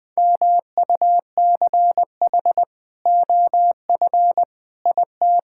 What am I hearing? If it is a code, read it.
Morse code, heard as a keyed tone, about 20 wpm, 700 Hz. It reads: MUCH OF IT
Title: Sentences with 2-Word Phrases - Spoken Beginning - 20wpm